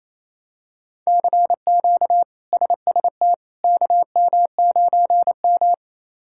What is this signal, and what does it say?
Morse code, heard as a keyed tone, about 28 wpm, 700 Hz. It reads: CQ SST KM9M
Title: K1USN SST QSO Practice - 28wpm